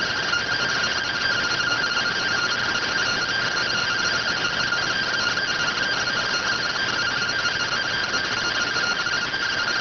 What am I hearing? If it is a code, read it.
Two-tone teleprinter signal, about 170 Hz shift.